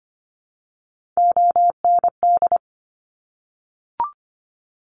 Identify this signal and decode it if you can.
Morse code, heard as a keyed tone, about 25 wpm, 700 Hz. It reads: OND
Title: Top Three Letter Combinations in Words 25wpm